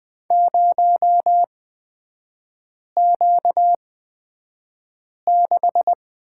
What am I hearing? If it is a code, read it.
Morse code, heard as a keyed tone, about 20 wpm, spaced as for 5 wpm, 700 Hz. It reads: 0Q6
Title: 236 Lesson 38e - Letter q - Three Characters 20wpm